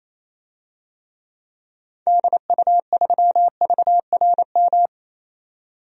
Morse code, heard as a keyed tone, about 28 wpm, 700 Hz. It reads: DU3VRM
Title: DXCC-100 Call Signs - 6 Characters - 28wpm